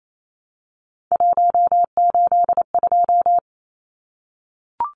Morse code, heard as a keyed tone, about 28 wpm, 700 Hz. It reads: 182
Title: Numbers - 3-Digit 28wpm